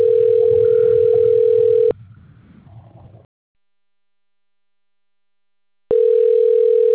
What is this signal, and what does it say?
Ringback tone (North America), 440 + 480 Hz.